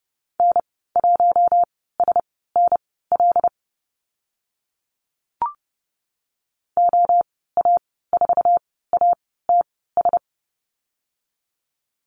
Morse code, heard as a keyed tone, about 30 wpm, spaced as for 17 wpm, 700 Hz. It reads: N1SNL OA4ATS
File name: Lesson 14g - Number 2 - Call Signs 30wpm